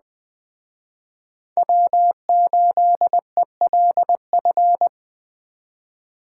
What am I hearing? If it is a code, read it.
Morse code, heard as a keyed tone, about 20 wpm, 700 Hz. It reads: W8ELF